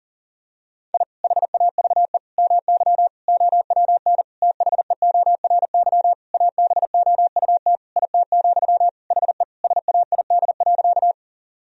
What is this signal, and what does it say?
Morse code, heard as a keyed tone, about 40 wpm, 700 Hz. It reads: I HAVE MY OWN THEORY ABOUT IT, HE SAID.
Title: Sentences from Top 1000 Words - Review Part 3 - 40wpm